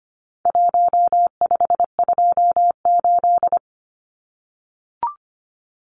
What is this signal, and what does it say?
Morse code, heard as a keyed tone, about 25 wpm, 700 Hz. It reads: 1528
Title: Numbers - Contest Style - 4-Digit 25wpm